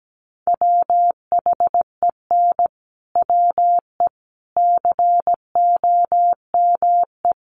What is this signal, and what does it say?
Morse code, heard as a keyed tone, about 17 wpm, 700 Hz. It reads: WHEN WE COME
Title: Sets of 4 Words 17wpm